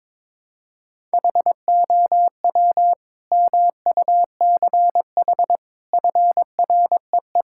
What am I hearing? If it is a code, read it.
Morse code, heard as a keyed tone, about 22 wpm, 700 Hz. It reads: HOW MUCH FREE
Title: Sentences from Top 1000 Words - Review Part 6 - 22wpm